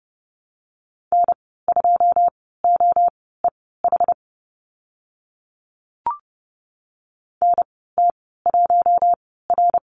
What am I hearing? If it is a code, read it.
Morse code, heard as a keyed tone, about 30 wpm, spaced as for 17 wpm, 700 Hz. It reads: N2OEH NT1R
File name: Lesson 14g - Number 2 - Call Signs 30wpm